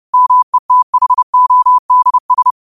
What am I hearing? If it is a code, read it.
Morse code, heard as a keyed tone, about 30 wpm, 1000 Hz. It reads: METHODS
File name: qotdfiles / 30wpm